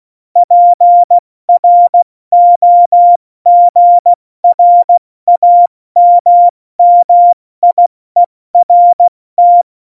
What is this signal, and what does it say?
Morse code, heard as a keyed tone, about 16 wpm, 700 Hz. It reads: PROGRAMMIERT
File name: Morsekurs / Witze / Witz 08. Januar 2019, 70 F80